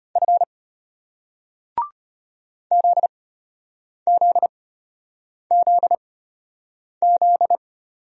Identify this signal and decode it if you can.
Morse code, heard as a keyed tone, about 35 wpm, spaced as for 8 wpm, 700 Hz. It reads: F ZZZZ